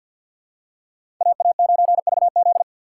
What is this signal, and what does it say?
Morse code, heard as a keyed tone, about 47 wpm, 700 Hz. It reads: AA9VZ
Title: USA Call Signs - 5 Characters - 4x Character Spacing - 50wpm